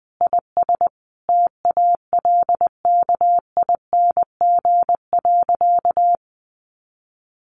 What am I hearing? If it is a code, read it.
Morse code, heard as a keyed tone, about 20 wpm, 700 Hz. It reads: IS TALKING.
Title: Additional Easy Phrases 20wpm